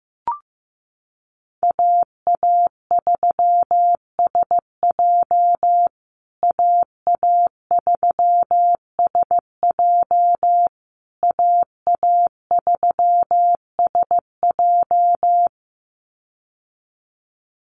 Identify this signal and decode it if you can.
Morse code, heard as a keyed tone, about 15 wpm, 700 Hz. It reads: AA3SJ AA3SJ AA3SJ
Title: USA Call Signs Repeated 3x 15wpm